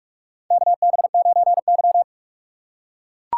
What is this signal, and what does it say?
Morse code, heard as a keyed tone, about 43 wpm, 700 Hz. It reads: KB9Y
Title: USA Call Signs - 4 Characters - 6x Character Spacing - 45wpm